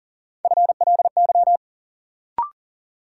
Morse code, heard as a keyed tone, about 40 wpm, 700 Hz. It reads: FLY
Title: Sentences from Top 500 Words - New Word Drill - 40wpm